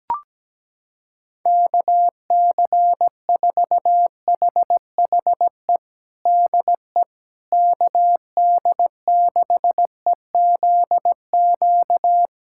Morse code, heard as a keyed tone, about 17 wpm, 700 Hz. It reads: KC4HHE DE KD6EZQ
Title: USA Callsign DE Callsign - 6 Characters - 17wpm